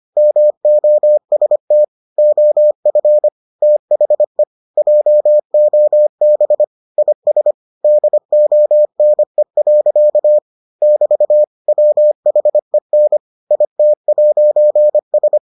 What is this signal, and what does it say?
Morse code, heard as a keyed tone, about 25 wpm, 600 Hz. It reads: MOST OF THE JOB IS DONE. <BT> WHEN IT'S